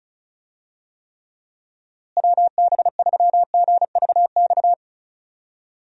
Morse code, heard as a keyed tone, about 36 wpm, 700 Hz. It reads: WB3GVX
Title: USA Call Signs - 6 Characters - Repeated 2x - 35wpm